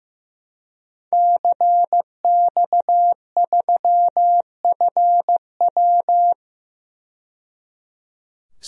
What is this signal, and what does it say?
Morse code, heard as a keyed tone, about 15 wpm, 700 Hz. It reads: CX3FW
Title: DXCC-100 Call Signs - 5 Characters - 15wpm